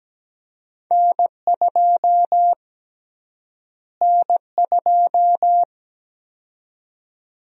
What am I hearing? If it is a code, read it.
Morse code, heard as a keyed tone, about 17 wpm, 700 Hz. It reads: N2 N2
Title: USA Call Sign Prefixes - Repeated 2x - 17wpm